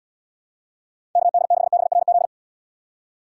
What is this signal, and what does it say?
Morse code, heard as a keyed tone, about 70 wpm, 700 Hz. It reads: KD6BRZ